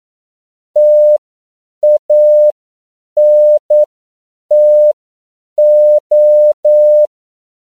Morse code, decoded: TANTO

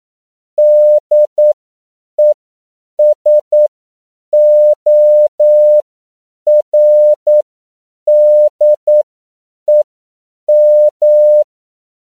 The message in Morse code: DESORDEM

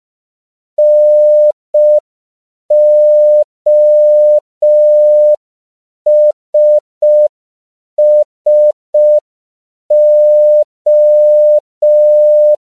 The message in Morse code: NOSSO

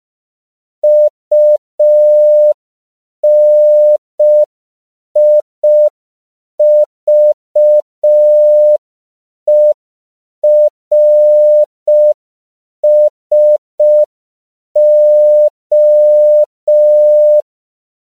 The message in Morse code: UNIVERSO

